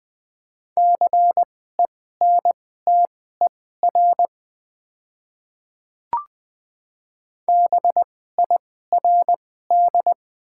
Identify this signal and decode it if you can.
Morse code, heard as a keyed tone, about 20 wpm, spaced as for 15 wpm, 690 Hz. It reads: CENTER BIRD
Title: 2x Character Spacing - Top 500 Words - 20wpm